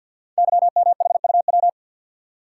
Morse code, heard as a keyed tone, about 52 wpm, 700 Hz. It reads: YMHUW
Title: Code Groups - 5 Letter:Numbers - 50wpm